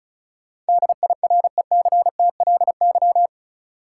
Morse code, heard as a keyed tone, about 36 wpm, 700 Hz. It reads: DIRECTLY
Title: Sentences from Top 800 Words - New Word Drill - 35wpm